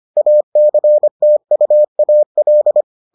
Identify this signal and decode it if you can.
Morse code, heard as a keyed tone, about 25 wpm, 600 Hz. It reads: ACTUAL